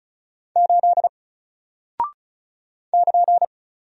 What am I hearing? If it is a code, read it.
Morse code, heard as a keyed tone, about 36 wpm, 700 Hz. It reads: 8 <KN>
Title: Single Character (Letter:Number:Punctuation:Prosign) Alt 35wpm